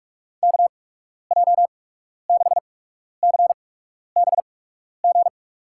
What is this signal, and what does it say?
Morse code, heard as a keyed tone, about 43 wpm, spaced as for 12 wpm, 700 Hz. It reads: KJ6CBG
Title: USA Call Signs - 8x Character Spacing - 45wpm